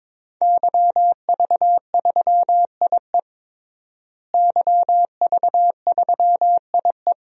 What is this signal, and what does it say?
Morse code, heard as a keyed tone, about 22 wpm, 700 Hz. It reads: YV3IE YV3IE
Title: DXCC-100 Call Signs - 5 Characters - Repeated 2x - 22wpm